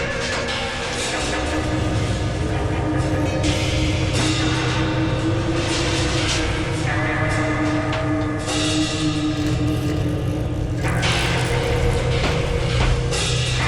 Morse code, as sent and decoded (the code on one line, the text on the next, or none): none